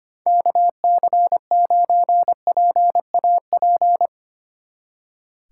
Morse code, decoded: KC9PAP

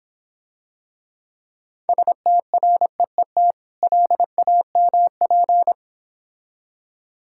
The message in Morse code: STREET LAMP

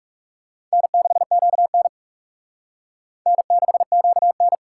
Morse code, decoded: N6QN N6QN

